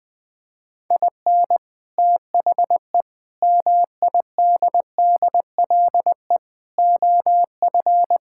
Morse code, heard as a keyed tone, about 21 wpm, 700 Hz. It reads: IN THE MIDDLE OF
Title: Build A Phrase - 5-Word Phrases - 20wpm